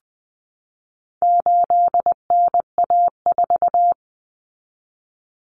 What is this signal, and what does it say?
Morse code, heard as a keyed tone, about 20 wpm, 700 Hz. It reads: 8NA4
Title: Common Cut Numbers - 4-Digit 20wpm